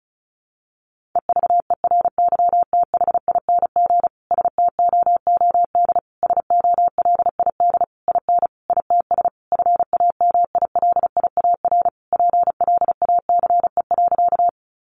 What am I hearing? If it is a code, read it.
Morse code, heard as a keyed tone, about 34 wpm, 700 Hz. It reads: EVERYTHING STOOD SOLID IN ITS FAMILIAR PLACE.
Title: Sentences from Top 1500 Words 35wpm